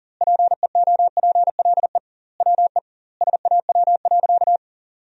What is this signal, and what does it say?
Morse code, heard as a keyed tone, about 40 wpm, 700 Hz. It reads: PEOPLE WE SAW.
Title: 2x Word-Spacing - Sentences from Top 300 Words - 40wpm